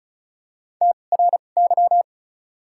Morse code, heard as a keyed tone, about 34 wpm, spaced as for 25 wpm, 700 Hz. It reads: TRY
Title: 2x Character Spacing - Top 200 Words - 35wpm